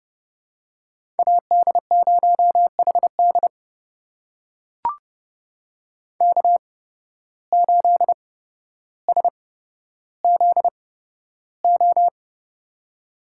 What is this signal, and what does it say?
Morse code, heard as a keyed tone, about 30 wpm, spaced as for 13 wpm, 700 Hz. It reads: AD0HD K8SZO
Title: USA Call Signs - 5 Characters - 8x Character Spacing - 30wpm